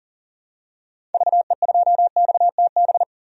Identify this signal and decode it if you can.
Morse code, heard as a keyed tone, about 40 wpm, 700 Hz. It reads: VE2XTB
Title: DXCC-100 Call Signs - 6 Characters - 40wpm